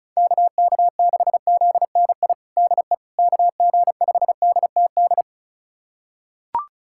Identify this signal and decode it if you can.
Morse code, heard as a keyed tone, about 36 wpm, 700 Hz. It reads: KK6ZNI DE KG5DTD
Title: USA Callsign DE Callsign - 6 Characters - 35wpm